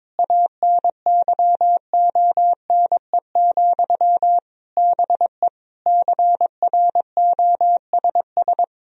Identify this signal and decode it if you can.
Morse code, heard as a keyed tone, about 22 wpm, 700 Hz. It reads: ANYONE, BE CROSS
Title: Sentences from Top 1000 Words - Review Part 1 - 22wpm